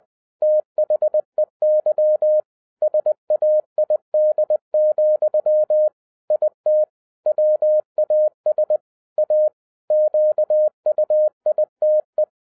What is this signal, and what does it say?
Morse code, heard as a keyed tone, about 20 wpm, 610 Hz. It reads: THEY SAID, IT WAS A QUITE